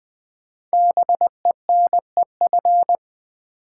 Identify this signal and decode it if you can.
Morse code, heard as a keyed tone, about 20 wpm, 700 Hz. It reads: BENEF
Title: Build A Word - 5 Letters At A Time - 10-Letter Words - 20wpm